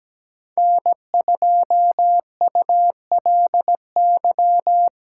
Morse code, heard as a keyed tone, about 17 wpm, 700 Hz. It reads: N2ULY
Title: USA Call Signs - 5 Characters - 3x Character Spacing - 17wpm